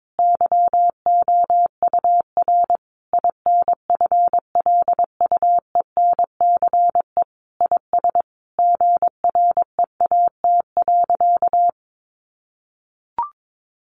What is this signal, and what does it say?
Morse code, heard as a keyed tone, about 22 wpm, 700 Hz. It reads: YOUR INFLUENCE IS GREAT.
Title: Sentences from Top 1100 Words 22wpm